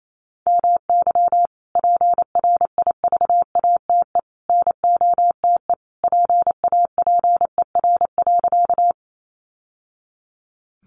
Morse code, heard as a keyed tone, about 28 wpm, 700 Hz. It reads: MY PRIVATE NOTE PAPER.
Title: Sentences from Top 1000 Words 28wpm